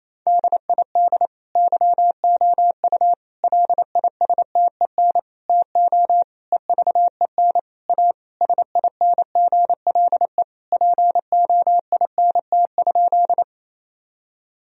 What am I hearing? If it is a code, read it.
Morse code, heard as a keyed tone, about 28 wpm, 700 Hz. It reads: DID YOU LISTEN TO EVEN A SINGLE POINT?
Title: Sentences from Top 1000 Words - Review Part 4 - 28wpm